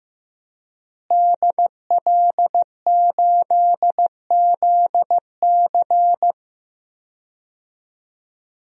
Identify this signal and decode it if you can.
Morse code, heard as a keyed tone, about 15 wpm, 700 Hz. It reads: DL8ZC
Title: DXCC-100 Call Signs - 5 Characters - Repeated 3x - 15wpm